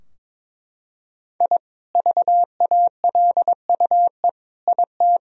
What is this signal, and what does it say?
Morse code, heard as a keyed tone, about 22 wpm, 700 Hz. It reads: I VALUE IT
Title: Sentences from Top 1000 Words - Review Part 3 - 22wpm